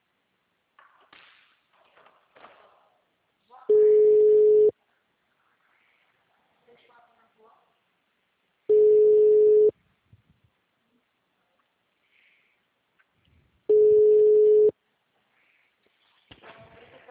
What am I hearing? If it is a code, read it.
Ringback tone (Europe, 425 Hz), with speech.